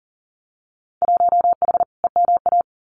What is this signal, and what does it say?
Morse code, heard as a keyed tone, about 40 wpm, 700 Hz. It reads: 1H EMA
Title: Winter Field Day Exchanges Encore 40wpm